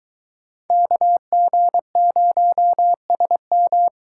Morse code, heard as a keyed tone, about 22 wpm, 700 Hz. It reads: KG0SM